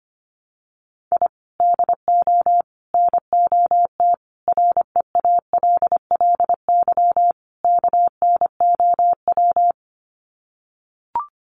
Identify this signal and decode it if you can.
Morse code, heard as a keyed tone, about 25 wpm, 700 Hz. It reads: I DO NOT REALLY KNOW
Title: Sets of 5 Words Encore 25wpm